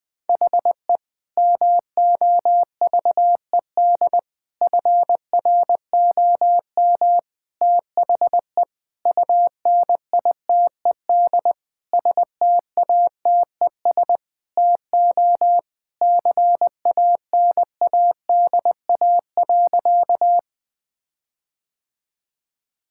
Morse code, heard as a keyed tone, about 20 wpm, 700 Hz. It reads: HE MOVED FROM THE UNITED STATES TO CANADA.
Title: Sentences with 4-Word Phrases - Spoken Beginning - 20wpm